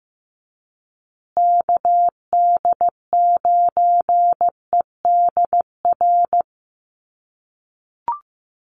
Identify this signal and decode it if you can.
Morse code, heard as a keyed tone, about 15 wpm, 700 Hz. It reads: KD9EDR
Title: USA Call Signs Repeated 2x 15wpm